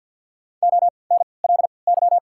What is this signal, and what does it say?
Morse code, heard as a keyed tone, about 54 wpm, spaced as for 29 wpm, 690 Hz. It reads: ONLY
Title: 3x Character Spacing - Top 200 Words - 50wpm